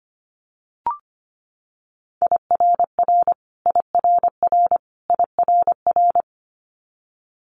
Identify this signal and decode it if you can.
Morse code, heard as a keyed tone, about 25 wpm, 700 Hz. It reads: IRR IRR IRR
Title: Multi-letter Phonemes Repeated 3x 25wpm